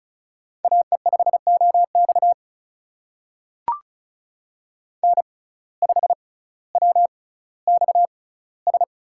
Morse code, heard as a keyed tone, about 36 wpm, spaced as for 18 wpm, 700 Hz. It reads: AE5OX N5WXS
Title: USA Call Signs - 5 Characters - 6x Character Spacing - 35wpm